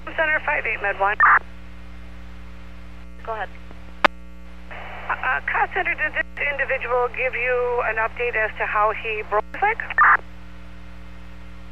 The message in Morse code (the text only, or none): none